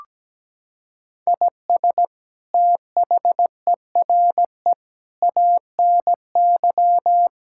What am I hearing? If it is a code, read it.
Morse code, heard as a keyed tone, about 17 wpm, 700 Hz. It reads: IS THERE ANY
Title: Sentences from Top 1000 Words - Review Part 5 - 17wpm